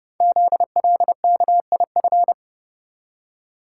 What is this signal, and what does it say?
Morse code, heard as a keyed tone, about 30 wpm, 700 Hz. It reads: ZLKIF
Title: Code Groups - 5 Letters - 30wpm